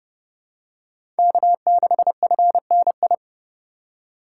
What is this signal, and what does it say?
Morse code, heard as a keyed tone, about 30 wpm, 700 Hz. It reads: K6FNI
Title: USA Call Signs - 5 Characters - 4x Character Spacing - 30wpm